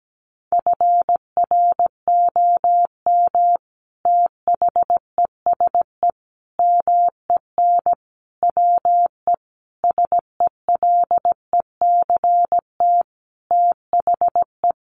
Morse code, heard as a keyed tone, about 17 wpm, 700 Hz. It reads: FROM THESE MEN WE SELECT THE
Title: Sentences from Top 1300 Words 17wpm